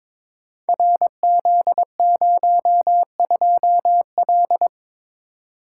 Morse code, heard as a keyed tone, about 22 wpm, 700 Hz. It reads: RZ02L
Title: Code Groups - 5 Letter:Numbers - 22wpm